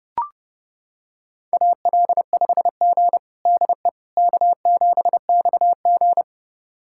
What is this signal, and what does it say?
Morse code, heard as a keyed tone, about 30 wpm, 700 Hz. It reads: AL5G DE K7XG
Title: USA Callsign DE Callsign - 4 Characters - 30wpm